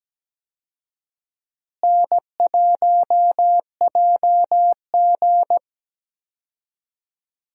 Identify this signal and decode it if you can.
Morse code, heard as a keyed tone, about 17 wpm, 700 Hz. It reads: N1JG
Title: USA Call Signs - 4 Characters - Repeated 3x - 17wpm